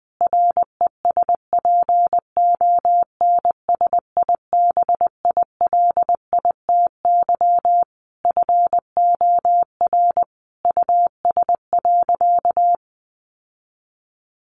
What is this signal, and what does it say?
Morse code, heard as a keyed tone, about 20 wpm, 700 Hz. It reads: RESPONSIBILITY FOR US.
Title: Sentences from Top 1500 Words 20wpm